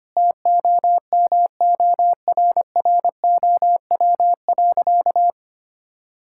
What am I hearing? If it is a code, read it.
Morse code, heard as a keyed tone, about 25 wpm, 700 Hz. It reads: TOMORROW.
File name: practice / 25wpm / Sentences with 4-Word Phrases - 25wpm